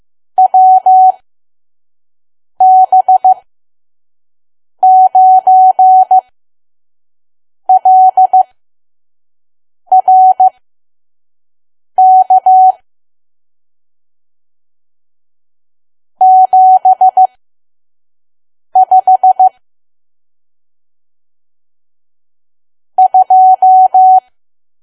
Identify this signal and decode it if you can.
Morse code, heard as a keyed tone, about 15 wpm, spaced as for 5 wpm, 750 Hz. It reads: WB9LRK 75 2
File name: Archive / 5 WPM files / PM